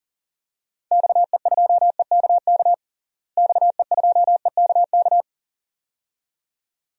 Morse code, heard as a keyed tone, about 37 wpm, 700 Hz. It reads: XE2EKK XE2EKK
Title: DXCC-100 Call Signs - 6 Characters - Repeated 2x - 40wpm